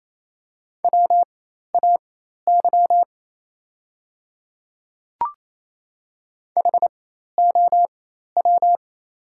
Morse code, heard as a keyed tone, about 28 wpm, spaced as for 13 wpm, 700 Hz. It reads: WAY HOW
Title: 4x Character Spacing - Top 100 Words - 28wpm